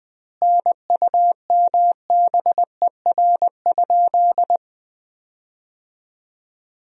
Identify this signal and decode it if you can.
Morse code, heard as a keyed tone, about 20 wpm, 700 Hz. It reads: NUMBER?
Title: 2x Word-Spacing - Sentences from Top 100 Words - 20wpm